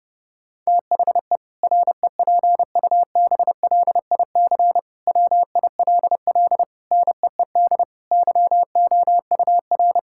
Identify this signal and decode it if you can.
Morse code, heard as a keyed tone, about 30 wpm, 700 Hz. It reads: THE REPUBLIC WILL NEED YOUR
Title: Sentences from Top 1600 Words 30wpm